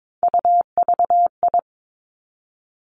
Morse code, heard as a keyed tone, about 22 wpm, 700 Hz. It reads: UVI
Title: Three Challenging Letter-Number Combinations 22wpm